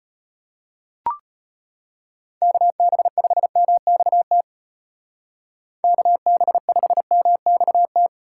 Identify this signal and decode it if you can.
Morse code, heard as a keyed tone, about 34 wpm, 700 Hz. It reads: KB5MXT KB5MXT